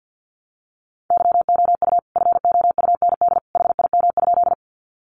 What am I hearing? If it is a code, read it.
Morse code, heard as a keyed tone, about 48 wpm, 700 Hz. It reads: YOU FOUND HIM?
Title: Sentences from Top 300 Words 50wpm